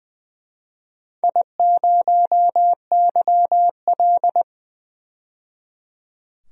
Morse code, heard as a keyed tone, about 21 wpm, 700 Hz. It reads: I0YL
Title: DXCC-100 Call Signs - 4 Characters - 20wpm